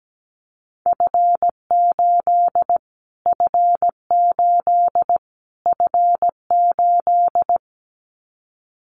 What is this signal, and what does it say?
Morse code, heard as a keyed tone, about 17 wpm, 700 Hz. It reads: F8 F8 F8
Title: Callsign Prefixes for Top 25 DXCC - Repeated 3x - 17wpm